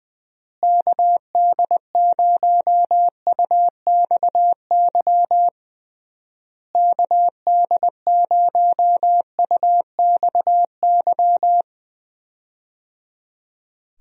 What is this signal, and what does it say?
Morse code, heard as a keyed tone, about 20 wpm, 700 Hz. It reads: KD0UXY KD0UXY
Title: USA Call Signs - 6 Characters - Repeated 3x - 20wpm